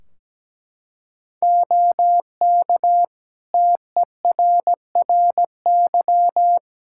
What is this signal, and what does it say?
Morse code, heard as a keyed tone, about 17 wpm, 700 Hz. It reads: OK TERRY
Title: Practice QSO - Exchange 4 - Ending - 17wpm